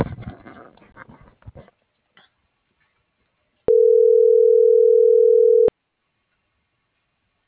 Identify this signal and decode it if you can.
Ringback tone (North America), 440 + 480 Hz.